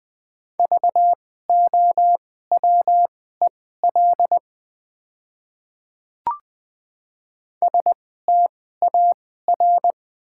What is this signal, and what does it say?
Morse code, heard as a keyed tone, about 20 wpm, spaced as for 14 wpm, 690 Hz. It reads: VOWEL STAR
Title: 2x Character Spacing - Top 500 Words - 20wpm